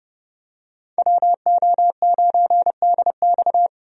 Morse code, heard as a keyed tone, about 30 wpm, 700 Hz. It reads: WO9DX
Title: USA Call Signs - 5 Characters - 30wpm